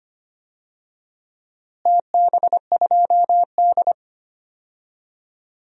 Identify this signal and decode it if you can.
Morse code, heard as a keyed tone, about 25 wpm, 700 Hz. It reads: TB2D
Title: DXCC-100 Call Signs - 4 Characters - Repeated 3x - 25wpm